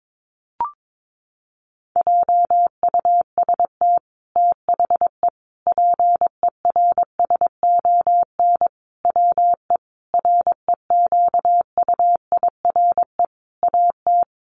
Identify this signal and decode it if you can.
Morse code, heard as a keyed tone, about 22 wpm, 700 Hz. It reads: JUST THE PERSON WE REQUIRE AT
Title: Sentences from Top 900 Words 22wpm